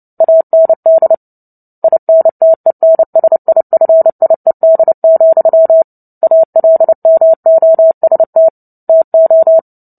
Morse code, heard as a keyed tone, about 30 wpm, 650 Hz. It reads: AND INTENSIFIED, ALMOST TO
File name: Book Two / 03-Chapter 3 - 29 WPM